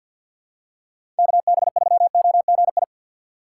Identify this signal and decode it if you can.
Morse code, heard as a keyed tone, about 46 wpm, 700 Hz. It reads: KB3OGI
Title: USA Call Signs - 4x Character Spacing - 50wpm